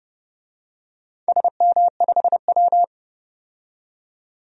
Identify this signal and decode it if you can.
Morse code, heard as a keyed tone, about 28 wpm, 700 Hz. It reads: SM5W
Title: DXCC-100 Call Signs - 4 Characters - 30wpm